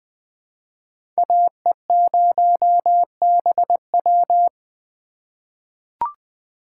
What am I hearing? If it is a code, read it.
Morse code, heard as a keyed tone, about 21 wpm, 700 Hz. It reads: AE0BW
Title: USA Call Signs - 5 Characters - 2x Character Spacing - 20wpm